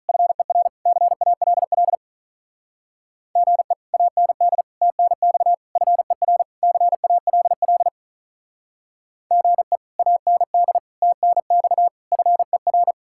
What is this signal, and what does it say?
Morse code, heard as a keyed tone, about 44 wpm, 700 Hz. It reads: FER CALL GE AND TNX FER CALL GE AND TNX FER
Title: Speed-Racing Common QSO Phrases Encore - 35wpm